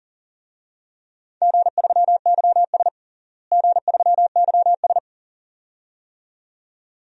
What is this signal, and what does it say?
Morse code, heard as a keyed tone, about 40 wpm, 700 Hz. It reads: G3YS G3YS
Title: DXCC-100 Call Signs - 4 Characters - Repeated 2x - 40wpm